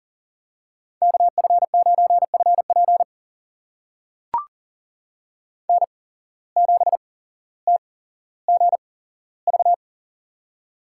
Morse code, heard as a keyed tone, about 40 wpm, spaced as for 17 wpm, 700 Hz. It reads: KF9FP N7TGV
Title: USA Call Signs - 5 Characters - 8x Character Spacing - 40wpm